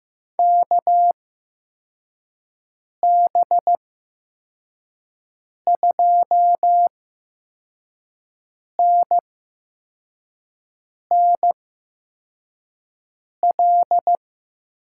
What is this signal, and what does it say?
Morse code, heard as a keyed tone, about 15 wpm, spaced as for 4 wpm, 700 Hz. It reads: KB2NNL
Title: USA Call Signs - 6 Characters - 8x Character Spacing - 15wpm